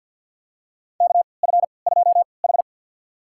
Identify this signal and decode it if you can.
Morse code, heard as a keyed tone, about 53 wpm, spaced as for 29 wpm, 700 Hz. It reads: KF2H
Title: USA Call Signs - 4 Characters - 3x Character Spacing - 50wpm